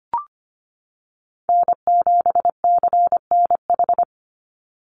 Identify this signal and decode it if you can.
Morse code, heard as a keyed tone, about 25 wpm, 700 Hz. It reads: N7CNH
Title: NA SOTA Chaser Callsigns 25wpm